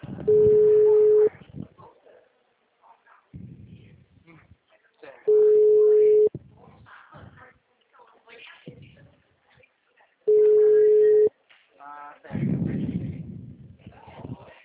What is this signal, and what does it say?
Ringback tone (Europe, 425 Hz), with speech.